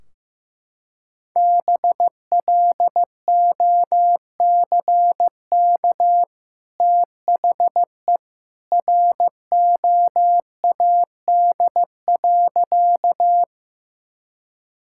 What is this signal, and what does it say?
Morse code, heard as a keyed tone, about 15 wpm, 700 Hz. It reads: BLOCK THE ROAD.
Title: Sentences from Top 1600 Words 15wpm